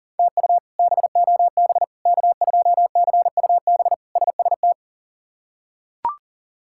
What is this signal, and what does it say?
Morse code, heard as a keyed tone, about 40 wpm, 700 Hz. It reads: TU BOB K2CUB SST
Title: K1USN SST QSO Practice - 40wpm